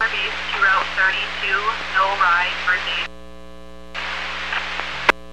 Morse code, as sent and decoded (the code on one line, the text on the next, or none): none